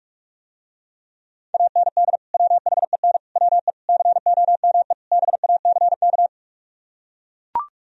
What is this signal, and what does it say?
Morse code, heard as a keyed tone, about 44 wpm, 700 Hz. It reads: AND WHEN WE COME BACK